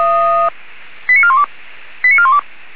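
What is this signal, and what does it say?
Five-tone selective call (ZVEI), heard as quick runs of five tones, tones of about 70 ms. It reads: 89412 89412
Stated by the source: ZVEI five-tone selective call received with an SDR, from the Feuerwehr Orlamünde fire brigade audio server.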